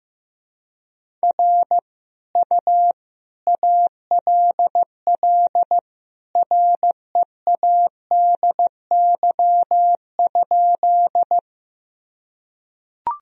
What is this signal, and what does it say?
Morse code, heard as a keyed tone, about 15 wpm, 700 Hz. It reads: R U ALL READY?
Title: Sentences from Top 400 Words and Abbreviations 15wpm